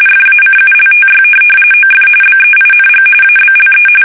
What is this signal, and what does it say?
RTTY, about 45.45 baud. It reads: FACULTY OF BEING ABLE
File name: fsk850 / benjamin-g-lamme